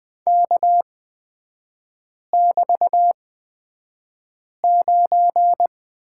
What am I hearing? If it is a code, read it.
Morse code, heard as a keyed tone, about 20 wpm, spaced as for 5 wpm, 700 Hz. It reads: K<BT>9
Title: 250 Lesson 40e - BK - Three Characters 20wpm